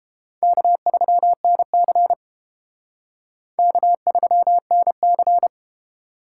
Morse code, heard as a keyed tone, about 29 wpm, 700 Hz. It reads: K3NC K3NC